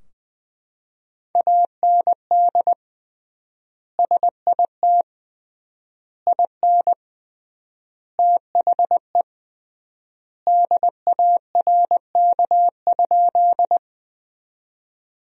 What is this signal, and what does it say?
Morse code, heard as a keyed tone, about 21 wpm, 700 Hz. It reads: AND SIT IN THE DARK?
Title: 3x Word-Spacing - Sentences from Top 500 Words - 20wpm